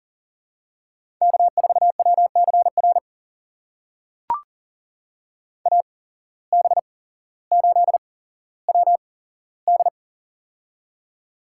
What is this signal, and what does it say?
Morse code, heard as a keyed tone, about 40 wpm, spaced as for 17 wpm, 700 Hz. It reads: K4WCR AB8WD
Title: USA Call Signs - 8x Character Spacing - 40wpm